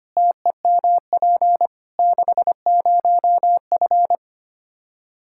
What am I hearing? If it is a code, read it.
Morse code, heard as a keyed tone, about 25 wpm, 700 Hz. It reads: TEMP 60F
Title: Practice QSO - Exchange 2 - Info - 25wpm